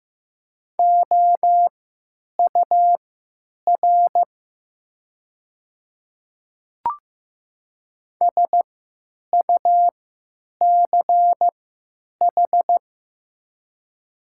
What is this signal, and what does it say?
Morse code, heard as a keyed tone, about 15 wpm, spaced as for 9 wpm, 700 Hz. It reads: OUR SUCH
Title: 3x Character Spacing - Top 200 Words - 15wpm